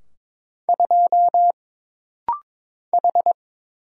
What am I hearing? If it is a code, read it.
Morse code, heard as a keyed tone, about 23 wpm, 700 Hz. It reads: 2 H
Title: Single Letter-Number Alt - Rapid-Fire 22wpm